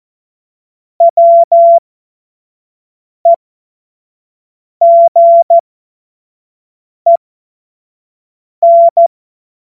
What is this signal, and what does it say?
Morse code, heard as a keyed tone, about 14 wpm, spaced as for 5 wpm, 680 Hz. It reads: WEGEN
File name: Morsekurs / Witze / Witz 03. August 2021, 25 F70